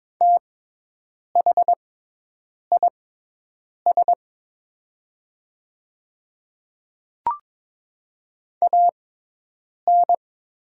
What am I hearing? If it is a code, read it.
Morse code, heard as a keyed tone, about 22 wpm, spaced as for 8 wpm, 690 Hz. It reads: THIS AN
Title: 6x Character Spacing - Top 200 Words - 22wpm